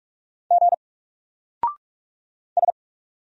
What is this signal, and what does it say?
Morse code, heard as a keyed tone, about 44 wpm, 700 Hz. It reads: G S
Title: Single Character (Letter:Number:Punctuation:Prosign) 45wpm